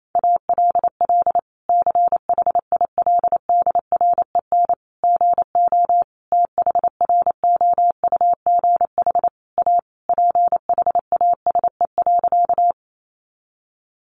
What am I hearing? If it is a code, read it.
Morse code, heard as a keyed tone, about 28 wpm, 700 Hz. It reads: ALL CHILDREN GO THROUGH A PHASE.
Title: Sentences from Top 1200 Words 28wpm